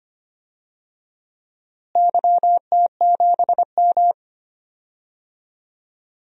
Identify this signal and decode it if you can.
Morse code, heard as a keyed tone, about 25 wpm, 700 Hz. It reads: YT7M
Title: DXCC-100 Call Signs - 4 Characters - Repeated 3x - 25wpm